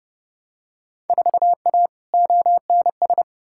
Morse code, heard as a keyed tone, about 32 wpm, 700 Hz. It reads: 4A ONS
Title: Field Day Exchanges - 30wpm